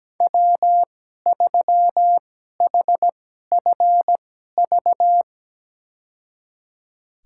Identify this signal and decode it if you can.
Morse code, heard as a keyed tone, about 17 wpm, spaced as for 12 wpm, 700 Hz. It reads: W3HFV
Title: USA Call Signs - 5 Characters - 2x Character Spacing - 17wpm